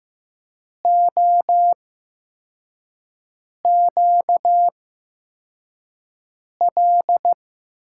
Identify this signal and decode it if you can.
Morse code, heard as a keyed tone, about 15 wpm, spaced as for 4 wpm, 700 Hz. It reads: OQL
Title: USA Call Signs - 6 Characters - 8x Character Spacing - 15wpm